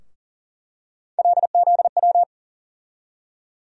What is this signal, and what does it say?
Morse code, heard as a keyed tone, about 40 wpm, 700 Hz. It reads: LZW
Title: USA Call Sign Postfixes - 3 Letters 40wpm